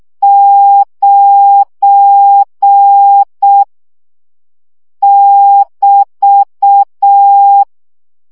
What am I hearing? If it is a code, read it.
Morse code, heard as a keyed tone, about 6 wpm, 800 Hz. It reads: 9 <BT>
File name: CW gyakorlas 9 5 baud